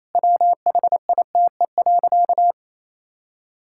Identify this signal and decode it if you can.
Morse code, heard as a keyed tone, about 28 wpm, 700 Hz. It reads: WHITE.
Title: Pangram Challenge v2 - 28wpm